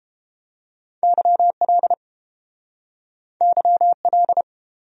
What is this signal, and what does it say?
Morse code, heard as a keyed tone, about 30 wpm, 700 Hz. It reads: YL YL